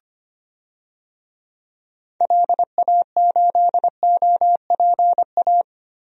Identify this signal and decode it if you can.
Morse code, heard as a keyed tone, about 25 wpm, 700 Hz. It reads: LA8OPA
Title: DXCC-100 Call Signs - 6 Characters - Repeated 2x - 25wpm